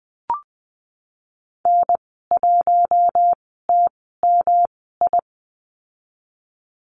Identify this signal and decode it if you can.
Morse code, heard as a keyed tone, about 20 wpm, spaced as for 15 wpm, 700 Hz. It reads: N1TMI
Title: 231 Lesson 37g - Letter x - Call Signs 20wpm